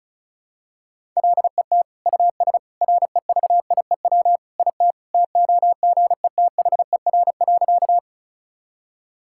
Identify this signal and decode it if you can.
Morse code, heard as a keyed tone, about 34 wpm, 700 Hz. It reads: LET US REVIEW IT TOGETHER.
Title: Sentences from Top 1000 Words - Review Part 6 - 35wpm